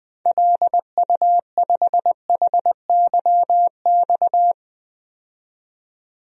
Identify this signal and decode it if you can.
Morse code, heard as a keyed tone, about 21 wpm, 700 Hz. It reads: LU5HYX
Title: DXCC-100 Call Signs - 6 Characters - 20wpm